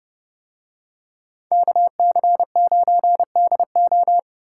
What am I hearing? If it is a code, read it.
Morse code, heard as a keyed tone, about 28 wpm, 700 Hz. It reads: KC9DO